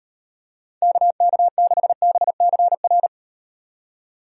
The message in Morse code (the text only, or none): KK6BCR